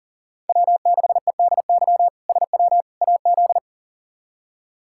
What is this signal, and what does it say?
Morse code, heard as a keyed tone, about 40 wpm, 700 Hz. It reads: W6EDY SW AZ
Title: K1USN SST QSO Practice - 40wpm